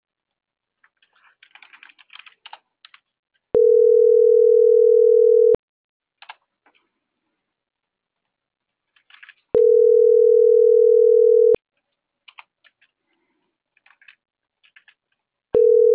Ringback tone (North America), 440 + 480 Hz.